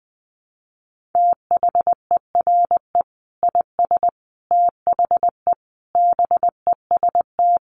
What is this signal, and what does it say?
Morse code, heard as a keyed tone, about 20 wpm, 700 Hz. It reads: THERE IS THE BEST
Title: Sentences from Top 1500 Words 20wpm